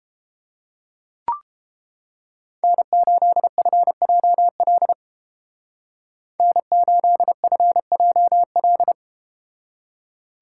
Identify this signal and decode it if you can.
Morse code, heard as a keyed tone, about 30 wpm, 700 Hz. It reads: N8FJL N8FJL